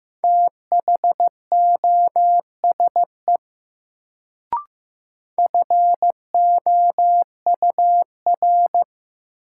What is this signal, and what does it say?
Morse code, heard as a keyed tone, about 15 wpm, 700 Hz. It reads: THOSE FOUR
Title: Sentences from Top 300 Words - New Word Drill - 15wpm